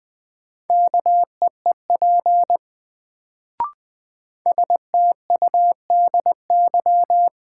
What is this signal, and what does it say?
Morse code, heard as a keyed tone, about 20 wpm, 700 Hz. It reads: KEEP STUDY
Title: Sentences from Top 300 Words - New Word Drill - 20wpm